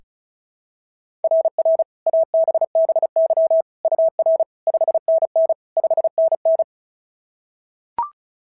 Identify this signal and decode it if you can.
Morse code, heard as a keyed tone, about 31 wpm, 650 Hz. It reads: RR ABBY UR 5NN 5NN
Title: SOTA QSO Practice 35wpm